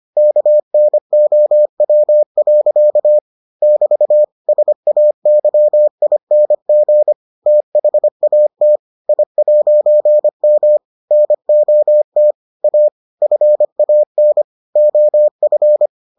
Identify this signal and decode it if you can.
Morse code, heard as a keyed tone, about 25 wpm, 600 Hz. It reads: KNOW. <BT> SAYING THAT I'M NOT A FAN OF